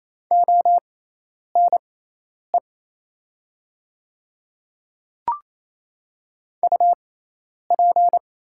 Morse code, heard as a keyed tone, about 27 wpm, spaced as for 10 wpm, 700 Hz. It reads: ONE UP
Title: 6x Character Spacing - Top 100 Words - 28wpm